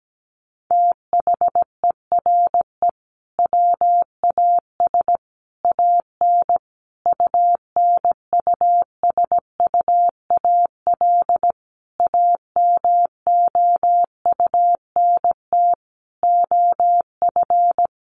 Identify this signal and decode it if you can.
Morse code, heard as a keyed tone, about 17 wpm, 700 Hz. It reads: THERE WAS AN UNUSUAL AMOUNT OF
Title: Sentences from Top 1400 Words 17wpm